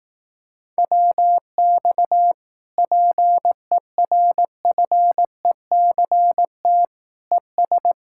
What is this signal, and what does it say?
Morse code, heard as a keyed tone, about 18 wpm, 700 Hz. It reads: WX PERFECT ES
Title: Speed-Racing Common QSO Phrases Encore - 15wpm